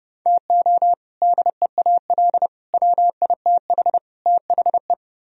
Morse code, heard as a keyed tone, about 30 wpm, 700 Hz. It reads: TO DEAL WITH THE